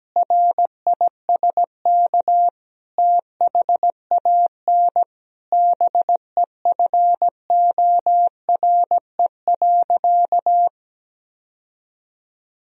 Morse code, heard as a keyed tone, about 17 wpm, 700 Hz. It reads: RISK THAN BEFORE.
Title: Sentences from Top 1000 Words - Review Part 4 - 17wpm